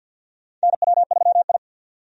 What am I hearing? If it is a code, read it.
Morse code, heard as a keyed tone, about 52 wpm, 700 Hz. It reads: NW3I
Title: USA Call Signs - 4 Characters - 50wpm